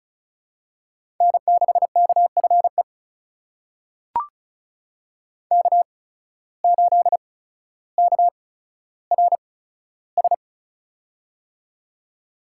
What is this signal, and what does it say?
Morse code, heard as a keyed tone, about 34 wpm, spaced as for 15 wpm, 700 Hz. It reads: N6KFE K8KRS
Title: USA Call Signs - 5 Characters - 8x Character Spacing - 35wpm